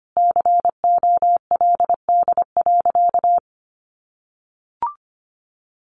Morse code, heard as a keyed tone, about 25 wpm, 700 Hz. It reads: COLD.
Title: Sentences from Top 1400 Words 25wpm